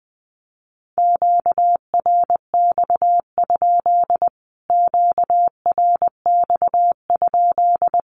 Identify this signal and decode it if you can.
Morse code, heard as a keyed tone, about 20 wpm, 700 Hz. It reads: QRX? QRX?
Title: Q-Codes Repeated 3x 20wpm